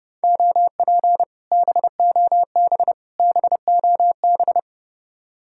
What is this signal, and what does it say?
Morse code, heard as a keyed tone, about 30 wpm, 700 Hz. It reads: OP BOB BOB